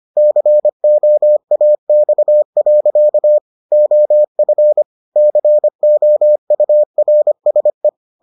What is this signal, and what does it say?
Morse code, heard as a keyed tone, about 25 wpm, 600 Hz. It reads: COAX. OF COURSE